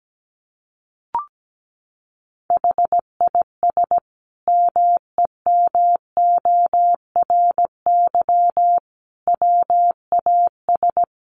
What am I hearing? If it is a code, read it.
Morse code, heard as a keyed tone, about 17 wpm, 700 Hz. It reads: HIS MEMORY WAS
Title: Sentences from Top 1300 Words 17wpm